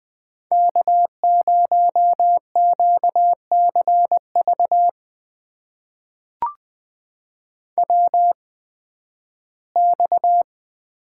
Morse code, heard as a keyed tone, about 20 wpm, spaced as for 13 wpm, 700 Hz. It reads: K0QCV WX
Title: USA Call Signs - 8x Character Spacing - 20wpm